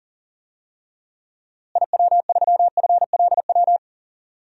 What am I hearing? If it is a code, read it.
Morse code, heard as a keyed tone, about 40 wpm, 700 Hz. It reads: IW3FLW
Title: DXCC-100 Call Signs - 6 Characters - Repeated 3x - 40wpm